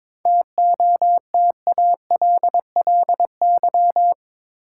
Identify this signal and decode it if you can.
Morse code, heard as a keyed tone, about 21 wpm, 700 Hz. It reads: TOTALLY
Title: Sentences from Top 1600 Words - New Word Drill Alt - 22wpm